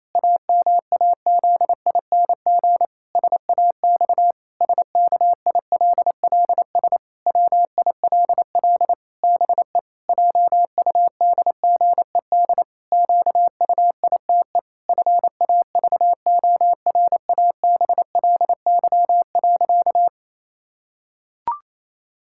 Morse code, heard as a keyed tone, about 28 wpm, 700 Hz. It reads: AMAZING SAX SKILLS WILL BE JUDGED QUITE FAVORABLY.
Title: Pangram Challenge v2 - 28wpm